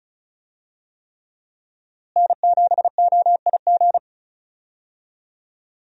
Morse code, heard as a keyed tone, about 36 wpm, 700 Hz. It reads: N7OIG
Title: USA Call Signs - 5 Characters - Repeated 3x - 35wpm